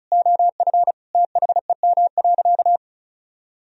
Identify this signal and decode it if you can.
Morse code, heard as a keyed tone, about 36 wpm, 700 Hz. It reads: OF THEM.
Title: Sentences from Top 1000 Words - Review Part 1 - 35wpm